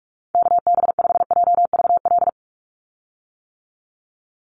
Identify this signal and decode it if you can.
Morse code, heard as a keyed tone, about 44 wpm, 700 Hz. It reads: KB5JVL
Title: USA Call Signs 45wpm